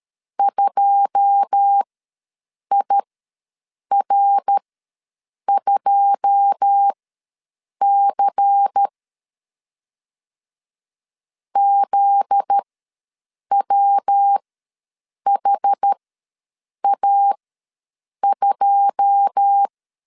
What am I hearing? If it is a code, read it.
Morse code, heard as a keyed tone, about 13 wpm, spaced as for 7 wpm, 800 Hz. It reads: 2IR2C ZWHA2
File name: LEZIONI / 9 2 ws 12wpm 3406